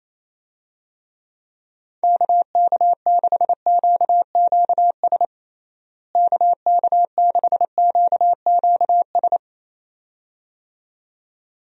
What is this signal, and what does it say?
Morse code, heard as a keyed tone, about 27 wpm, 700 Hz. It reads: KK6QQS KK6QQS
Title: USA Call Signs - 6 Characters - Repeated 2x - 28wpm